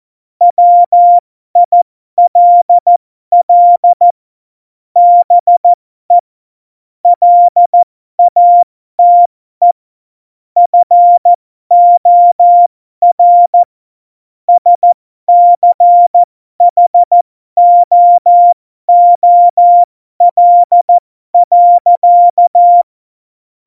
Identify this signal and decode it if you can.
Morse code, heard as a keyed tone, about 14 wpm, spaced as for 12 wpm, 700 Hz. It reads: WILL BE LATE FOR SCHOOL.
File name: Morsekurs / Witze